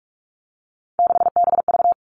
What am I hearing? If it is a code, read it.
Morse code, heard as a keyed tone, about 46 wpm, 700 Hz. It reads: 6BV
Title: Three Challenging Letter-Number Combinations 45wpm